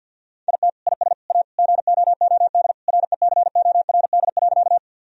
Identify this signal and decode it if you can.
Morse code, heard as a keyed tone, about 50 wpm, 700 Hz. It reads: IT IS A GOOD RECORD.
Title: Sentences from Top 1000 Words - Review Part 3 - 50wpm